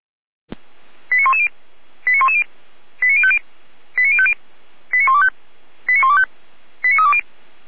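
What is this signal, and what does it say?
Five-tone selective call (ZVEI), heard as quick runs of five tones, tones of about 70 ms. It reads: 89110 89110 89050 89050 89125 89125 89320
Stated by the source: ZVEI five-tone call received by SDR, from the feuerwehr-orlamuende.org fire brigade audio server.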